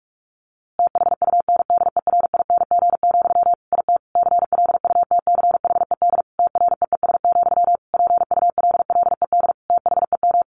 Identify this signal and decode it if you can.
Morse code, heard as a keyed tone, about 44 wpm, 700 Hz. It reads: THUNDERING, IT CLUTCHED TREES, PULLED THEM
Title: A Sound Of Thunder 45wpm